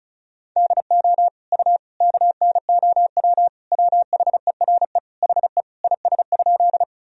Morse code, heard as a keyed tone, about 36 wpm, 700 Hz. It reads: DO U KNOW WHERE HE IS?
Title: Sentences from Top 300 Words and Abbreviations 35wpm